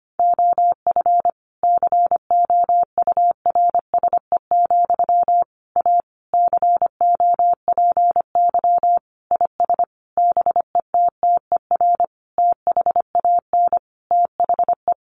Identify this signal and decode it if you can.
Morse code, heard as a keyed tone, about 25 wpm, 700 Hz. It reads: OF COURSE, A COPY IS BETTER THAN THE
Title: Sentences from Top 1400 Words 25wpm